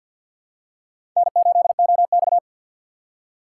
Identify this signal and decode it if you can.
Morse code, heard as a keyed tone, about 52 wpm, 690 Hz. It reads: N8OX
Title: USA Call Signs - 4 Characters - 6x Character Spacing - 50wpm